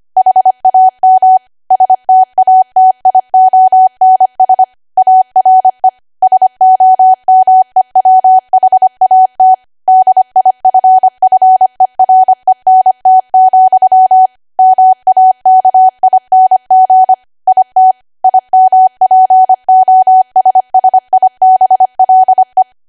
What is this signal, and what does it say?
Morse code, heard as a keyed tone, about 25 wpm, 750 Hz. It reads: HAM STATIONS ARE SOMEWHAT DIFFERENT, MAKING IT IMPOSSIBLE
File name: Archive / 25 WPM / 25WPM